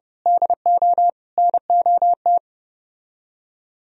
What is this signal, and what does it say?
Morse code, heard as a keyed tone, about 28 wpm, 700 Hz. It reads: DO NOT